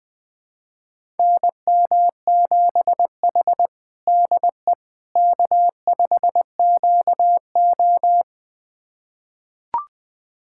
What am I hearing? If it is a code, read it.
Morse code, heard as a keyed tone, about 20 wpm, 700 Hz. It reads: NM7H DE K5QO
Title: USA Callsign DE Callsign - 4 Characters - 20wpm